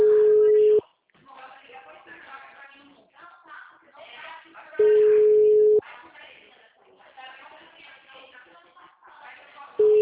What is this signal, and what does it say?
Ringback tone (Europe, 425 Hz), with speech.